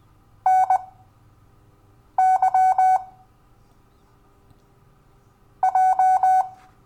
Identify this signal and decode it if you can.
Morse code, heard as a keyed tone, about 20 wpm, spaced as for 6 wpm, 750 Hz. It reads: NY J
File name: cw / Lesson12 YV